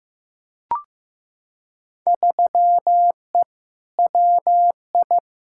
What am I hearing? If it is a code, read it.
Morse code, heard as a keyed tone, about 15 wpm, 700 Hz. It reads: 3E WI
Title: Field Day Exchanges Encore - 15wpm